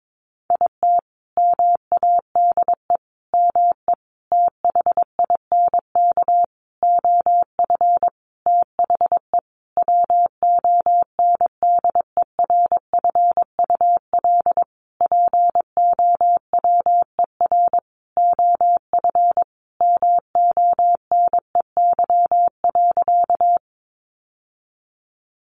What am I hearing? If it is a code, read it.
Morse code, heard as a keyed tone, about 22 wpm, 700 Hz. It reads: IT MADE ME THINK OF THE WONDERFUL POWER OF MONEY.
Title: Sentences from Top 1100 Words 22wpm